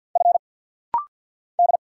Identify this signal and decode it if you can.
Morse code, heard as a keyed tone, about 50 wpm, 700 Hz. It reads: F D
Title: Single Letter-Number - Rapid-Fire 50wpm